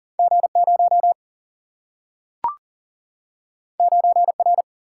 Morse code, heard as a keyed tone, about 39 wpm, 700 Hz. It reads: G0 9R
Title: Single-Letter Single-Number 40wpm